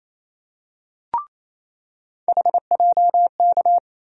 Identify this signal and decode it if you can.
Morse code, heard as a keyed tone, about 28 wpm, 700 Hz. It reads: HJK
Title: USA Call Sign Postfixes - 3 Letters 28wpm